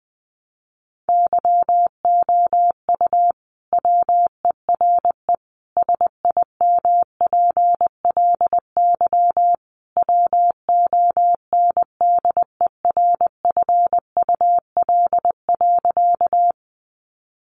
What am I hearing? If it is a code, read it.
Morse code, heard as a keyed tone, about 20 wpm, 700 Hz. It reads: YOU WERE SIMPLY WONDERFUL.
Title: Sentences from Top 1100 Words 20wpm